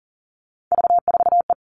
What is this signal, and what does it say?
Morse code, heard as a keyed tone, about 40 wpm, 700 Hz. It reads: V4E
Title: Three Challenging Letter-Number Combinations 40wpm